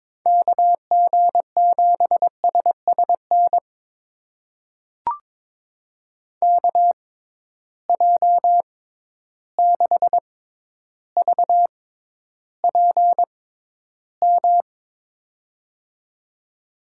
Morse code, heard as a keyed tone, about 22 wpm, spaced as for 11 wpm, 700 Hz. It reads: KG7SSN KJ6VPM
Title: USA Call Signs - 6 Characters - 6x Character Spacing - 22wpm